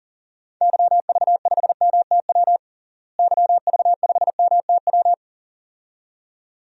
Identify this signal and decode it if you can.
Morse code, heard as a keyed tone, about 40 wpm, 700 Hz. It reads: YV5MTW YV5MTW
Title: DXCC-100 Call Signs - 6 Characters - Repeated 2x - 40wpm